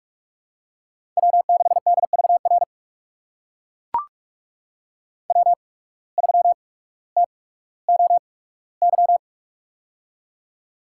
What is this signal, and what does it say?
Morse code, heard as a keyed tone, about 41 wpm, spaced as for 19 wpm, 700 Hz. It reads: W6DVR W3TOY
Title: USA Call Signs - 5 Characters - 8x Character Spacing - 45wpm